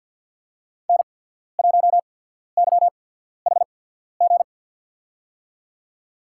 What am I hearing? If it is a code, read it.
Morse code, heard as a keyed tone, about 53 wpm, spaced as for 14 wpm, 700 Hz. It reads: N1YHG
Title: USA Call Signs - 5 Characters - 8x Character Spacing - 50wpm